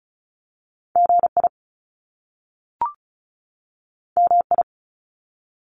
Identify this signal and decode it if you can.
Morse code, heard as a keyed tone, about 36 wpm, 700 Hz. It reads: G I M I
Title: Top Two Letter Combinations in Words 35wpm